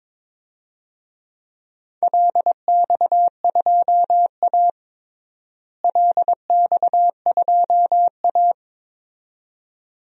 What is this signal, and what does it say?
Morse code, heard as a keyed tone, about 21 wpm, 700 Hz. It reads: LX2A LX2A
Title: DXCC-100 Call Signs - 4 Characters - Repeated 2x - 22wpm